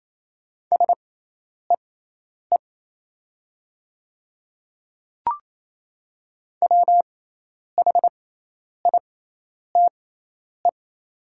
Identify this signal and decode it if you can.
Morse code, heard as a keyed tone, about 27 wpm, spaced as for 10 wpm, 700 Hz. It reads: SEE WHITE
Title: 6x Character Spacing - Top 300 Words - 28wpm